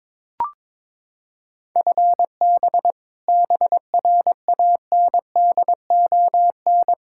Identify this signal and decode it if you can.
Morse code, heard as a keyed tone, about 22 wpm, 700 Hz. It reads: FB BRANDON
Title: Practice QSO - Exchange 3 - More Info - 22wpm